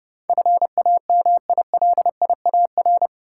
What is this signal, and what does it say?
Morse code, heard as a keyed tone, about 32 wpm, 700 Hz. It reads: FAMILIAR
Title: Sentences from Top 1000 Words - New Word Drill - 30wpm